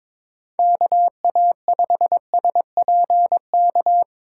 Morse code, heard as a keyed tone, about 23 wpm, 700 Hz. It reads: KA5SPK
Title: Speed Racing - USA 2x3 Call Signs 22wpm